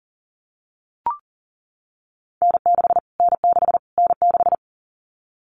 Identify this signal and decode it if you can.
Morse code, heard as a keyed tone, about 40 wpm, 700 Hz. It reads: N6 N6 N6
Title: Callsign Prefixes for Top 25 DXCC - Repeated 3x - 40wpm